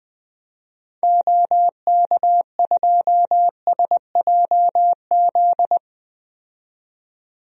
Morse code, heard as a keyed tone, about 21 wpm, 700 Hz. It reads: OK2SJZ
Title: DXCC-100 Call Signs - 6 Characters - 20wpm